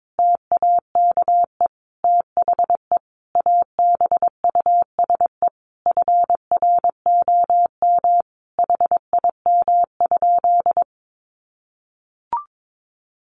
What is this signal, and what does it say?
Morse code, heard as a keyed tone, about 22 wpm, 700 Hz. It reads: TAKE THE ABUSE FROM HIM?
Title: Sentences from Top 1300 Words 22wpm